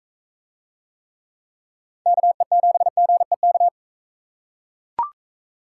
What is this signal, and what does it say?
Morse code, heard as a keyed tone, about 42 wpm, 700 Hz. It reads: KE7GEK